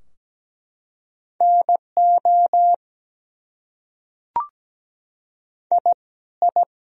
Morse code, heard as a keyed tone, about 17 wpm, 700 Hz. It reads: NO I I
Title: Top 100 Words Repeated 17wpm